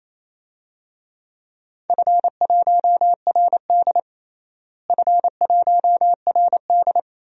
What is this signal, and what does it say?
Morse code, heard as a keyed tone, about 27 wpm, 700 Hz. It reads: F1RD F1RD
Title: DXCC-100 Call Signs - 4 Characters - Repeated 3x - 28wpm